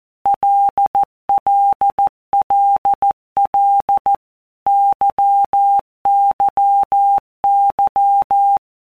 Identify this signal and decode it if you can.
Morse code, heard as a keyed tone, about 14 wpm, 800 Hz. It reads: LLLL YYY
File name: Basic Letter Groups By WPM's / Groups@13 / LFCYQ@13